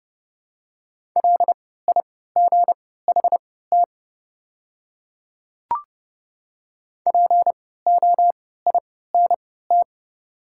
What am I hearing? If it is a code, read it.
Morse code, heard as a keyed tone, about 30 wpm, spaced as for 17 wpm, 700 Hz. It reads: LIGHT POINT
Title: 3x Character Spacing - Top 300 Words - 30wpm